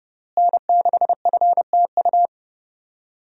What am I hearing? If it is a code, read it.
Morse code, heard as a keyed tone, about 30 wpm, 700 Hz. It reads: N6FTU
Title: USA Call Signs - 5 Characters - Repeated 3x - 30wpm